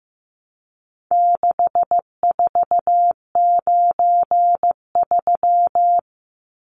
Morse code, heard as a keyed tone, about 14 wpm, 700 Hz. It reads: 6493
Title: Numbers - 4-Digit 15wpm